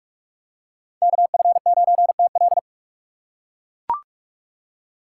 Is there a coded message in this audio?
Yes: Morse code from 1.0 to 2.6 s, heard as a keyed tone, about 39 wpm, 700 Hz. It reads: KF9TL